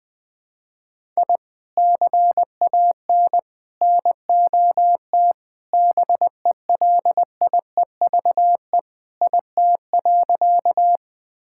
Morse code, heard as a keyed tone, about 20 wpm, 690 Hz. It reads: I CAN NOT BELIEVE IT.
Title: Sentences with 4-Word Phrases - Spoken Beginning - 20wpm